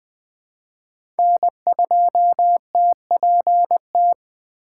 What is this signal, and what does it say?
Morse code, heard as a keyed tone, about 20 wpm, 700 Hz. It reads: N2TPT
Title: USA Call Signs - 5 Characters - 3x Character Spacing - 20wpm